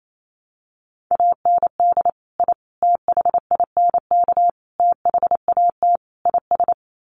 Morse code, heard as a keyed tone, about 28 wpm, 700 Hz. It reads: AND I THINK THAT IS
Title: Sets of 5 Words Encore2 28wpm